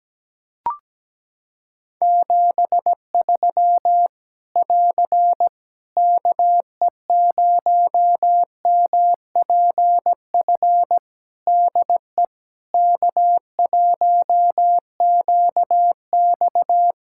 Morse code, heard as a keyed tone, about 17 wpm, 700 Hz. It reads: 73 <AR> KE0MPF DE K1QX
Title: Practice QSO - Exchange 4 - Ending - 17wpm